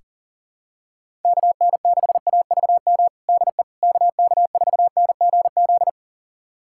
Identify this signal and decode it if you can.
Morse code, heard as a keyed tone, about 40 wpm, 700 Hz. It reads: KN6AVM DE KK4NGZ
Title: USA Callsign DE Callsign 40wpm